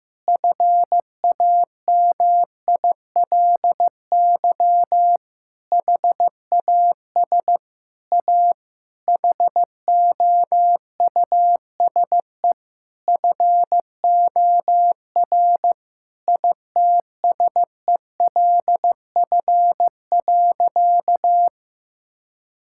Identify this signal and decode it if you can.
Morse code, heard as a keyed tone, about 16 wpm, 700 Hz. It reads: FAMILY HAS A HOUSE FOR ITSELF.
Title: Sentences from Top 1000 Words - Review Part 5 - 15wpm